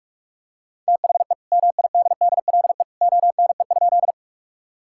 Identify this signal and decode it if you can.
Morse code, heard as a keyed tone, about 41 wpm, 700 Hz. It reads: THE MIDDLE ONE?
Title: Sentences from Top 1000 Words - Review Part 6 - 45wpm